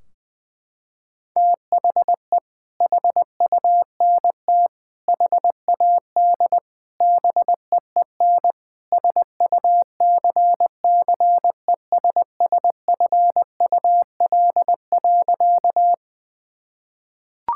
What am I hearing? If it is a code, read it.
Morse code, heard as a keyed tone, about 20 wpm, 700 Hz. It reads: THE HUNT HAD BEEN SUCCESSFUL.
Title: Sentences from Top 1600 Words 20wpm